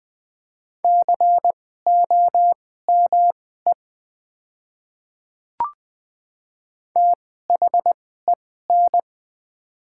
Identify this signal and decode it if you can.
Morse code, heard as a keyed tone, about 20 wpm, spaced as for 14 wpm, 700 Hz. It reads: COME THEN
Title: 2x Character Spacing - Top 100 Words - 20wpm